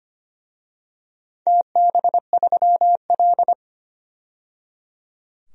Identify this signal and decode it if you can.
Morse code, heard as a keyed tone, about 25 wpm, 700 Hz. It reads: TB3L
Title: DXCC-100 Call Signs - 4 Characters - 25wpm